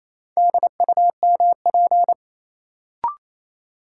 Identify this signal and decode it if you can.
Morse code, heard as a keyed tone, about 28 wpm, 700 Hz. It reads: DUMP
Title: Sentences from Top 1300 Words - New Word Drill Alt - 28wpm